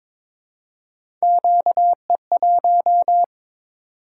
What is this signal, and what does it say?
Morse code, heard as a keyed tone, about 23 wpm, 700 Hz. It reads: QE1
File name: practice / 22wpm / Three Letter-Number Combinations - 22wpm